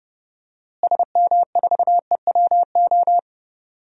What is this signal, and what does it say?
Morse code, heard as a keyed tone, about 30 wpm, 700 Hz. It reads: SM4EWO
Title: DXCC-100 Call Signs - 6 Characters - Repeated 3x - 30wpm